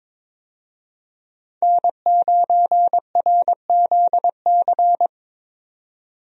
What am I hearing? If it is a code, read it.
Morse code, heard as a keyed tone, about 22 wpm, 700 Hz. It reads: N9RZC
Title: USA Call Signs - 5 Characters - 22wpm